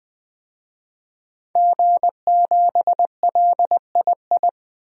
Morse code, heard as a keyed tone, about 20 wpm, 700 Hz. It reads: G7LII
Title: DXCC-100 Call Signs - 5 Characters - 20wpm